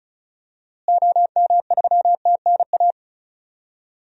Morse code, heard as a keyed tone, about 36 wpm, 700 Hz. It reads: OM3TNA
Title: DXCC-100 Call Signs - 6 Characters - Repeated 3x - 35wpm